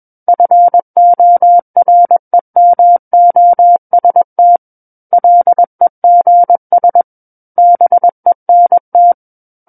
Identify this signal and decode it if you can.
Morse code, heard as a keyed tone, about 21 wpm, 700 Hz. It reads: FOREMOST LEGS BENT
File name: Book One / 12-Chapter 12 - 21 WPM